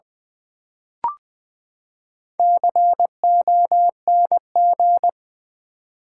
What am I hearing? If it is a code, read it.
Morse code, heard as a keyed tone, about 20 wpm, 700 Hz. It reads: CONG
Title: Build A Word - 4 Letters At A Time - 12-Letter Words - 20wpm